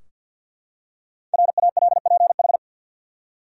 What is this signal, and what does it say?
Morse code, heard as a keyed tone, about 52 wpm, 700 Hz. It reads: RALPH
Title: USA Common Names - 50wpm